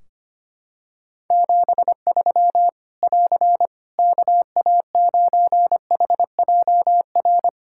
Morse code, heard as a keyed tone, about 25 wpm, 700 Hz. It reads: 73 <AR> KA9HJR
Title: Practice QSO - Exchange 4 - Ending - 25wpm